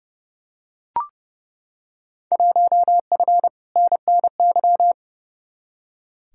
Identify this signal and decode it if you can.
Morse code, heard as a keyed tone, about 30 wpm, 700 Hz. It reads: 1F NNY
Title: Field Day Exchanges - 30wpm